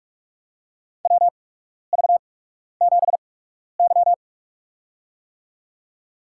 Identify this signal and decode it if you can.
Morse code, heard as a keyed tone, about 46 wpm, spaced as for 12 wpm, 700 Hz. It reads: WV7Y
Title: USA Call Signs - 4 Characters - 8x Character Spacing - 45wpm